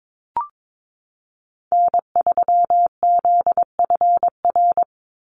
Morse code, heard as a keyed tone, about 22 wpm, 700 Hz. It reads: N3ZFR
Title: USA Call Signs 22wpm